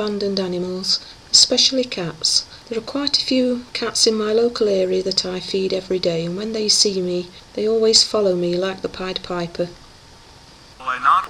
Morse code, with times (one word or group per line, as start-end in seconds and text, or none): none